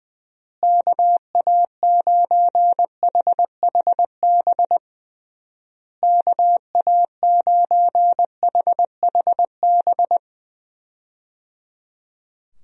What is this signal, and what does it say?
Morse code, heard as a keyed tone, about 20 wpm, 700 Hz. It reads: KA9HHB KA9HHB
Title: USA Call Signs - 6 Characters - Repeated 3x - 20wpm